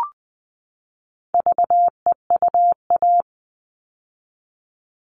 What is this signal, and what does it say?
Morse code, heard as a keyed tone, about 20 wpm, 700 Hz. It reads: VEUA
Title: All Cut Numbers - 4-Digit 20wpm